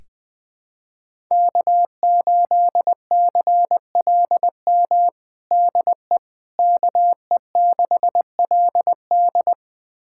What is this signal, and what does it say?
Morse code, heard as a keyed tone, about 20 wpm, 700 Hz. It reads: K8CLM DE KE6LD
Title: USA Callsign DE Callsign - 5 Characters - 20wpm